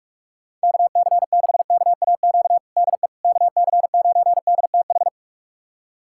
Morse code, heard as a keyed tone, about 46 wpm, 700 Hz. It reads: KC6KAQ DE KC9DTH
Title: USA Callsign DE Callsign - 6 Characters - 45wpm